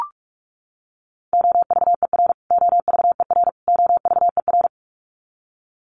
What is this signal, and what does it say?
Morse code, heard as a keyed tone, about 44 wpm, 700 Hz. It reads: OVER OVER OVER
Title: Top 200 Words Repeated 45wpm